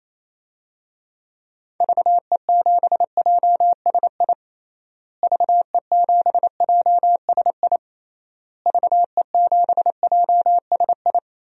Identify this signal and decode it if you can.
Morse code, heard as a keyed tone, about 28 wpm, 700 Hz. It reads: VE7JSI VE7JSI VE7JSI
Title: DXCC-100 Call Signs - 6 Characters - Repeated 3x - 28wpm